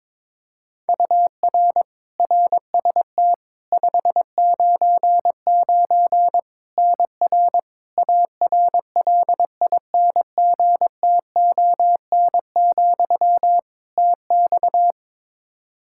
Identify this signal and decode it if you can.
Morse code, heard as a keyed tone, about 22 wpm, 700 Hz. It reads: UR RST 599 NR ARLINGTON, TX
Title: Practice QSO - Exchange 1 - Beginning - 22wpm